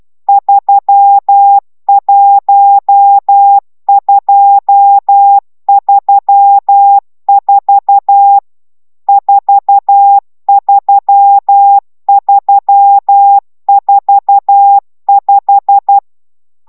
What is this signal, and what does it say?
Morse code, heard as a keyed tone, about 12 wpm, 800 Hz. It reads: 31234 43345